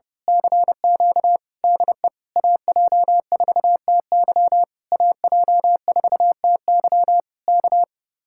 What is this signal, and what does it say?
Morse code, heard as a keyed tone, about 28 wpm, 700 Hz. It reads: CQ DE AJ4TY AJ4TY K
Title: Speed-Racing Common QSO Phrases - 30wpm